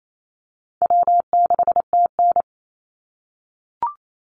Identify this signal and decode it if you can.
Morse code, heard as a keyed tone, about 28 wpm, 700 Hz. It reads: W6TN
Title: Easy Call Signs - 28wpm